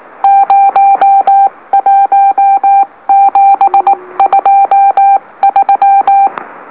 Morse code, heard as a keyed tone, about 19 wpm, spaced as for 16 wpm, 790 Hz. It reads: 01723
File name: M04- -2345utc-LOLO